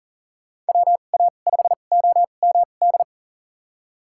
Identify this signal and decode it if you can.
Morse code, heard as a keyed tone, about 40 wpm, spaced as for 29 wpm, 700 Hz. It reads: WA5OMD
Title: USA Call Signs - 6 Characters - 2x Character Spacing - 40wpm